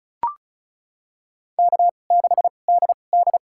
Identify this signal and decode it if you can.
Morse code, heard as a keyed tone, about 34 wpm, spaced as for 25 wpm, 700 Hz. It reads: K6DD
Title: USA Call Signs - 4 Characters - 2x Character Spacing - 35wpm